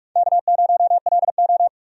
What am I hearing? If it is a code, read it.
Morse code, heard as a keyed tone, about 42 wpm, 700 Hz. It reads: K0LO
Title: USA Call Signs - 4 Characters - 45wpm